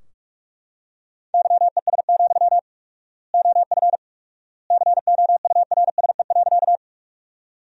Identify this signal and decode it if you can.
Morse code, heard as a keyed tone, about 41 wpm, 700 Hz. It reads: YES, OF COURSE.
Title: 4x Word-Spacing - Sentences from Top 500 Words - 45wpm